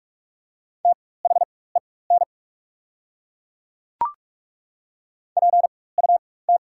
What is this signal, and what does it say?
Morse code, heard as a keyed tone, about 48 wpm, spaced as for 21 wpm, 690 Hz. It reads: THEN PUT
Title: 4x Character Spacing - Top 300 Words - 45wpm